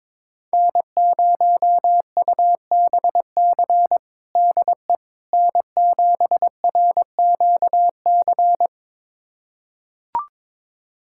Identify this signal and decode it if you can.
Morse code, heard as a keyed tone, about 23 wpm, 700 Hz. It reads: N0UBC DE N7RQC
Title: USA Callsign DE Callsign - 5 Characters - 22wpm